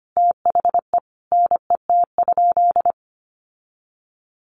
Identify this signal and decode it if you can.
Morse code, heard as a keyed tone, about 25 wpm, 700 Hz. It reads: THE NET?
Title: Sentences from Top 1400 Words 25wpm